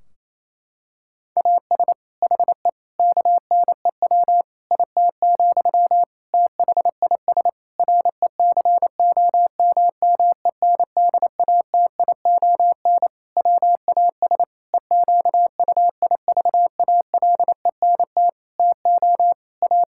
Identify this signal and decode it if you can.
Morse code, heard as a keyed tone, about 28 wpm, 700 Hz. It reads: AS HE KNEW IT, THIS RECOMMENDATION WAS EQUIVALENT TO A
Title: Sentences from Top 1600 Words 28wpm